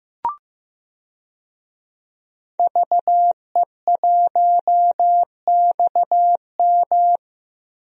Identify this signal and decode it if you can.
Morse code, heard as a keyed tone, about 15 wpm, 700 Hz. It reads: VE1XM
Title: DXCC-100 Call Signs - 5 Characters - Repeated 2x - 15wpm